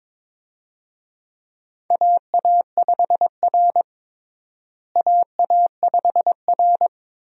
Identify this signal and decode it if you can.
Morse code, heard as a keyed tone, about 21 wpm, 700 Hz. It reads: AA5R AA5R
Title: USA Call Signs - 4 Characters - Repeated 3x - 22wpm